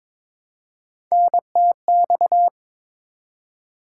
Morse code, heard as a keyed tone, about 22 wpm, 700 Hz. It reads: NTX
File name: ARRL Sections v2 - 22wpm